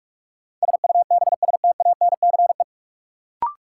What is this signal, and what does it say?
Morse code, heard as a keyed tone, about 46 wpm, 700 Hz. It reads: SUBSTANCE
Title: Sentences from Top 1500 Words - New Word Drill Alt - 45wpm